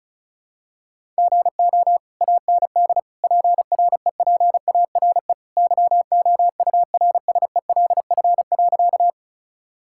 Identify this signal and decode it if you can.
Morse code, heard as a keyed tone, about 34 wpm, 700 Hz. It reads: GO AND PREPARE YOURSELF.
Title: Sentences from Top 1000 Words - Review Part 6 - 35wpm